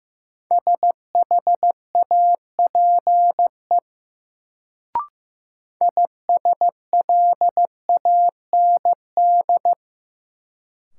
Morse code, heard as a keyed tone, about 15 wpm, 700 Hz. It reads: SHAPE ISLAND
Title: Sentences from Top 500 Words - New Word Drill - 15wpm